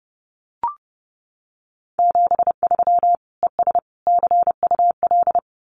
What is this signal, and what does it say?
Morse code, heard as a keyed tone, about 30 wpm, 700 Hz. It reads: 73 ES CUL
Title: Common QSO Phrases Encore 30wpm